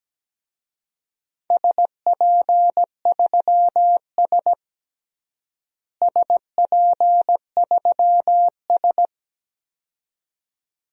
Morse code, heard as a keyed tone, about 17 wpm, 700 Hz. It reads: SP3S SP3S
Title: DXCC-100 Call Signs - 4 Characters - Repeated 2x - 17wpm